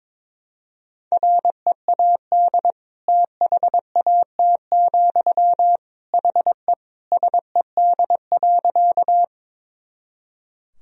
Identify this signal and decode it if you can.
Morse code, heard as a keyed tone, about 22 wpm, 700 Hz. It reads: READ THAT, HE SED.
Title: Sentences from Top 200 Words and Abbreviations 22wpm